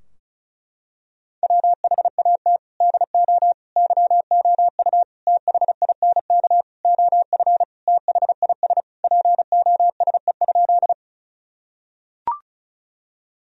Morse code, heard as a keyed tone, about 36 wpm, 700 Hz. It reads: WHAT DO YOU THINK OF THIS POSE?
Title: Sentences from Top 1000 Words - Review Part 2 - 35wpm